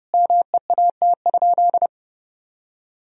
Morse code, heard as a keyed tone, about 30 wpm, 700 Hz. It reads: MEAT?
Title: Sentences from Top 1000 Words - Review Part 3 - 30wpm